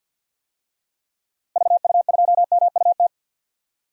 Morse code, heard as a keyed tone, about 51 wpm, 700 Hz. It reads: VU2MUT
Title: DXCC-100 Call Signs - 6 Characters - Repeated 2x - 50wpm